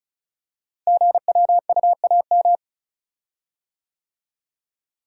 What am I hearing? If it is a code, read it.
Morse code, heard as a keyed tone, about 36 wpm, 700 Hz. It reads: GWUAM
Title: Code Groups - 5 Letters - 35wpm